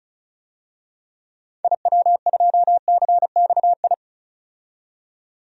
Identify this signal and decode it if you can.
Morse code, heard as a keyed tone, about 34 wpm, 700 Hz. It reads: IW2CXI
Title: DXCC-100 Call Signs - 6 Characters - 35wpm